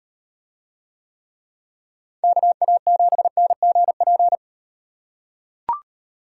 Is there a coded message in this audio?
Yes: Morse code from 2.2 to 4.4 s, heard as a keyed tone, about 38 wpm, 700 Hz. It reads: KA7NGP